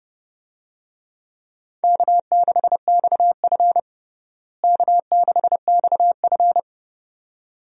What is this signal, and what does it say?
Morse code, heard as a keyed tone, about 30 wpm, 700 Hz. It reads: K6XF K6XF
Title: USA Call Signs - 4 Characters - Repeated 2x - 30wpm